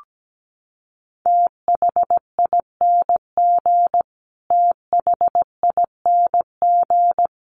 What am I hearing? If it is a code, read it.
Morse code, heard as a keyed tone, about 17 wpm, 700 Hz. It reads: THING THING
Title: Top 300 Words Repeated 17wpm